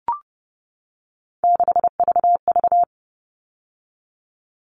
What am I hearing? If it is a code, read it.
Morse code, heard as a keyed tone, about 30 wpm, 700 Hz. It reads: 6VV
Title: All Cut Numbers - 3-Digit 30wpm